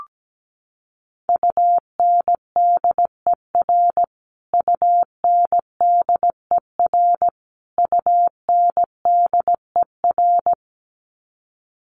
Morse code, heard as a keyed tone, about 17 wpm, 700 Hz. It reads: UNDER UNDER UNDER
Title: Top 300 Words Repeated 17wpm